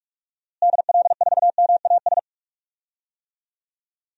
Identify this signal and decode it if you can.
Morse code, heard as a keyed tone, about 46 wpm, 700 Hz. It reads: DL4MAS
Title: DXCC-100 Call Signs - 6 Characters - Repeated 3x - 45wpm